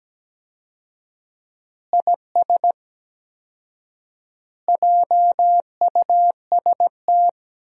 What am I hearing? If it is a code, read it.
Morse code, heard as a keyed tone, about 17 wpm, 700 Hz. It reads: IS JUST
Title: 4x Word-Spacing - Sentences from Top 200 Words - 17wpm